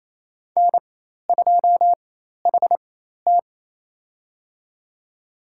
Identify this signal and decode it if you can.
Morse code, heard as a keyed tone, about 28 wpm, spaced as for 13 wpm, 700 Hz. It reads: N2HT
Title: USA Call Signs - 4x Character Spacing - 28wpm